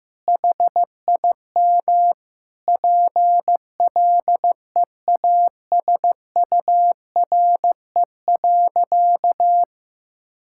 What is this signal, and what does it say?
Morse code, heard as a keyed tone, about 15 wpm, 700 Hz. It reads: HIM PLEASURE.
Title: Sentences from Top 1600 Words 15wpm